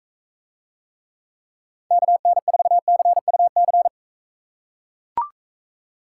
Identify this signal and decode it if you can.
Morse code, heard as a keyed tone, about 42 wpm, 700 Hz. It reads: KN4CUC